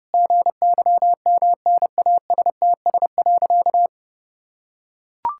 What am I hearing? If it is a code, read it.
Morse code, heard as a keyed tone, about 32 wpm, 700 Hz. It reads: GYMNASTS.
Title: Pangram Challenge v2 - 30wpm